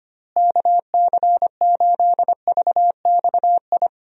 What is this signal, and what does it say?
Morse code, heard as a keyed tone, about 25 wpm, 700 Hz. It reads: KC8VXI
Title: USA Call Signs - 6 Characters - Repeated 3x - 25wpm